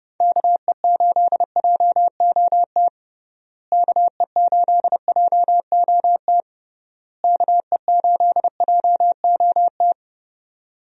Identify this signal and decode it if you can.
Morse code, heard as a keyed tone, about 32 wpm, 700 Hz. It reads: KE8JOT KE8JOT KE8JOT
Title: USA Call Signs - 6 Characters - Repeated 3x - 30wpm